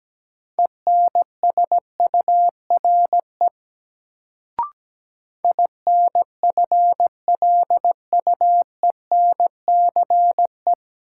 Morse code, heard as a keyed tone, about 17 wpm, 700 Hz. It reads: ENSURE INFLUENCE
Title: Sentences from Top 1100 Words - New Word Drill - 17wpm